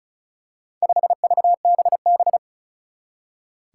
Morse code, heard as a keyed tone, about 36 wpm, 700 Hz. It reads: 5VBB
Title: Four Challenging Letter-Number Combinations 35wpm